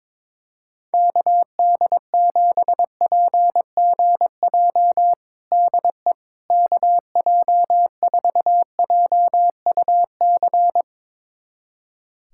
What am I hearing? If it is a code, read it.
Morse code, heard as a keyed tone, about 22 wpm, 700 Hz. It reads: KD7PGJ DE KJ4JUC
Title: USA Callsign DE Callsign 22wpm